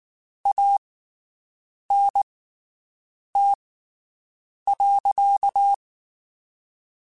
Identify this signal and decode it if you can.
Morse code, heard as a keyed tone, about 19 wpm, spaced as for 7 wpm, 800 Hz. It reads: ANT.
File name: QSOs Spaced